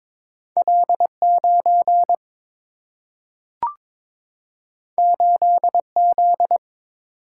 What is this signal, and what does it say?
Morse code, heard as a keyed tone, about 22 wpm, 700 Hz. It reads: L9 8Z
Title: Single-Letter Single-Number 22wpm